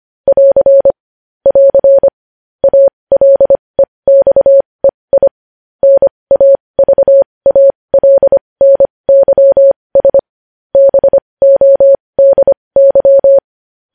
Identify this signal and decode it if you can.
Morse code, heard as a keyed tone, about 25 wpm, spaced as for 20 wpm, 550 Hz. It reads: <AR> <AR> ALEXEI NAVALNYS BODY